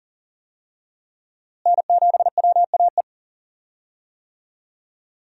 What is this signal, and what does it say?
Morse code, heard as a keyed tone, about 40 wpm, 700 Hz. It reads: N7WAE
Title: USA Call Signs - 5 Characters - Repeated 2x - 40wpm